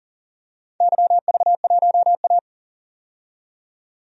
Morse code, heard as a keyed tone, about 40 wpm, 700 Hz. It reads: YV1A
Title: DXCC-100 Call Signs - 4 Characters - 40wpm